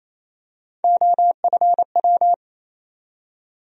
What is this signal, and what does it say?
Morse code, heard as a keyed tone, about 28 wpm, 700 Hz. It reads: OFW
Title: USA Call Sign Postfixes - 3 Letters 28wpm